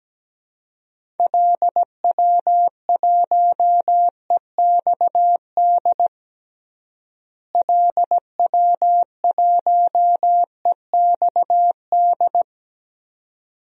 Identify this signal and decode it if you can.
Morse code, heard as a keyed tone, about 17 wpm, 700 Hz. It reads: LW1EXD LW1EXD
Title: DXCC-100 Call Signs - 6 Characters - Repeated 2x - 17wpm